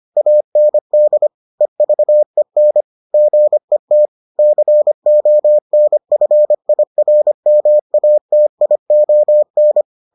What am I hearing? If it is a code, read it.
Morse code, heard as a keyed tone, about 25 wpm, 600 Hz. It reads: AND EVEN GET CONFIRMATION